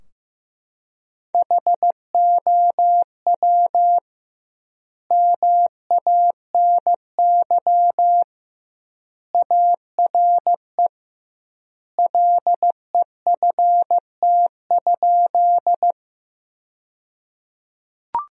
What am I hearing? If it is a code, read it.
Morse code, heard as a keyed tone, about 15 wpm, 700 Hz. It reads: HOW MANY ARE LEFT?
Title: 2x Word-Spacing - Sentences from Top 300 Words - 15wpm